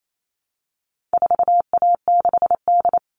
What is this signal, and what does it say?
Morse code, heard as a keyed tone, about 28 wpm, 700 Hz. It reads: 4A6D
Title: All Cut Numbers - 4-Digit 28wpm